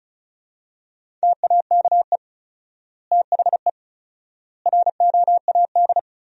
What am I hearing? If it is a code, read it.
Morse code, heard as a keyed tone, about 36 wpm, 700 Hz. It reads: TAKE THE ROAD
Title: 4x Word-Spacing - Sentences from Top 400 Words - 35wpm